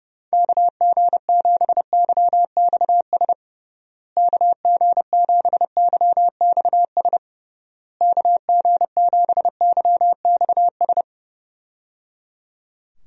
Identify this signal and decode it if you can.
Morse code, heard as a keyed tone, about 30 wpm, 700 Hz. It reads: KG7YXS KG7YXS KG7YXS
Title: USA Call Signs - 6 Characters - Repeated 3x - 30wpm